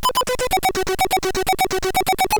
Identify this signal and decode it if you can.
There is no code.